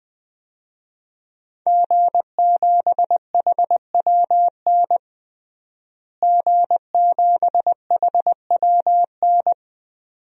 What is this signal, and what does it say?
Morse code, heard as a keyed tone, about 21 wpm, 700 Hz. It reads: G7HWN G7HWN
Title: DXCC-100 Call Signs - 5 Characters - Repeated 2x - 20wpm